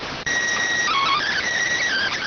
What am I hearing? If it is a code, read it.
SSTV (slow-scan television), PD 290.